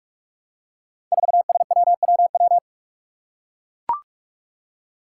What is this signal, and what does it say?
Morse code, heard as a keyed tone, about 44 wpm, 700 Hz. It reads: 4SWWW
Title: Five Challenging Letter-Number Combinations 45wpm